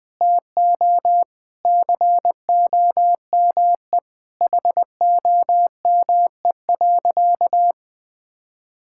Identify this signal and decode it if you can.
Morse code, heard as a keyed tone, about 20 wpm, 700 Hz. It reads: TO COME HOME.
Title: Sentences with 4-Word Phrases - 20wpm